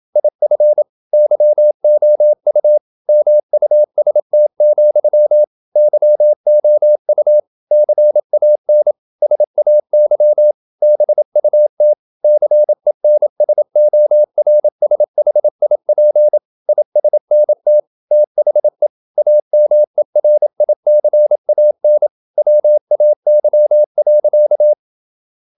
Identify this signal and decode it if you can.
Morse code, heard as a keyed tone, about 27 wpm, 600 Hz. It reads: IF YOU MUST, YOU CAN SAY BUT CENSORSHIP ISNT THE AMERICAN WAY.
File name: Never 0027wpm